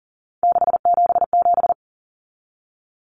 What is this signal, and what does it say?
Morse code, heard as a keyed tone, about 40 wpm, 700 Hz. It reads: 677
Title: Numbers - 3-Digit 40wpm